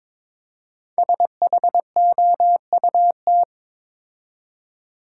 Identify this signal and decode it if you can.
Morse code, heard as a keyed tone, about 22 wpm, 700 Hz. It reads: SHOUT
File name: 5 Letter Words - First 3 Letters Given - 22wpm